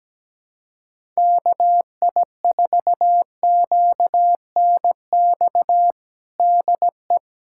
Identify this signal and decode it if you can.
Morse code, heard as a keyed tone, about 17 wpm, 700 Hz. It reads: KI4QNX DE
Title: USA Callsign DE Callsign - 6 Characters - 17wpm